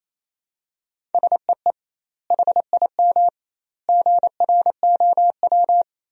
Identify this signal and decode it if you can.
Morse code, heard as a keyed tone, about 28 wpm, 700 Hz. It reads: SEE HIM GROW
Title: 2x Word-Spacing - Sentences from Top 300 Words - 28wpm